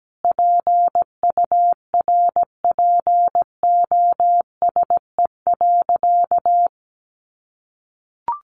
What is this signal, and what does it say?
Morse code, heard as a keyed tone, about 17 wpm, 700 Hz. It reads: PURPOSE.
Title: Sentences from Top 1500 Words 17wpm